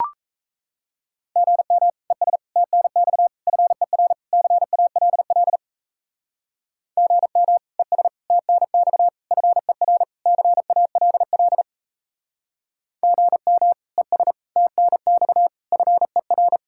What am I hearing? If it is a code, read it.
Morse code, heard as a keyed tone, about 36 wpm, 700 Hz. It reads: GM ES TNX FER CALL GM ES TNX FER CALL GM ES TNX FER